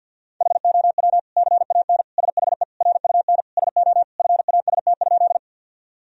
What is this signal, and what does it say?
Morse code, heard as a keyed tone, about 50 wpm, 700 Hz. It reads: HOW CAN SHE RUN SO FAST?
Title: Sentences from Top 1000 Words - Review Part 2 - 50wpm